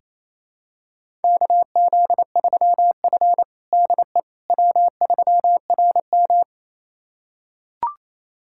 Morse code, heard as a keyed tone, about 28 wpm, 700 Hz. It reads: KZ3F DE W3RM
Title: USA Callsign DE Callsign - 4 Characters - 28wpm